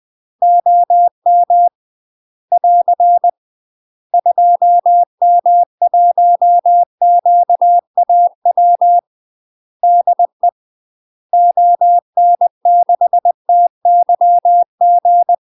Morse code, heard as a keyed tone, about 20 wpm, 700 Hz. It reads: OM <AR> 2M1QAW DE ON6TYG
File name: UK MASTER QSOs TRAINING / 20WPM / UK MASTER QSOs 0047